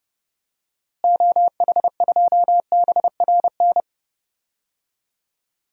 Morse code, heard as a keyed tone, about 32 wpm, 700 Hz. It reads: OH2BRN
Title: DXCC-100 Call Signs - 6 Characters - Repeated 3x - 30wpm